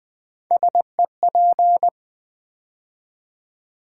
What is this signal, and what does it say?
Morse code, heard as a keyed tone, about 20 wpm, 700 Hz. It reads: SEP
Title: Build A Word - 3 Letters At A Time - 8-Letter Words - 20wpm